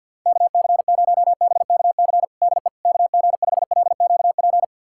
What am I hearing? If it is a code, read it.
Morse code, heard as a keyed tone, about 49 wpm, 700 Hz. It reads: KC0BKC DE KG5LQP
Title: USA Callsign DE Callsign - 6 Characters - 50wpm